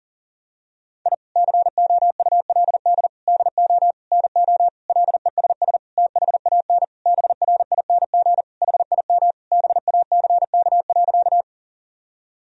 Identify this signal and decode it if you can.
Morse code, heard as a keyed tone, about 40 wpm, 700 Hz. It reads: I COULD DO NO LESS THAN BRING HIM BACK.
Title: Sentences from Top 1000 Words - Review Part 3 - 40wpm